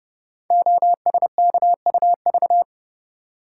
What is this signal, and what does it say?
Morse code, heard as a keyed tone, about 30 wpm, 700 Hz. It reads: OSKUV
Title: Code Groups - 5 Letters - 30wpm